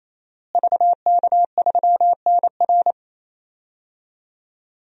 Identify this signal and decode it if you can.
Morse code, heard as a keyed tone, about 28 wpm, 700 Hz. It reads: VK3NR
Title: DXCC-100 Call Signs - 5 Characters - Repeated 3x - 28wpm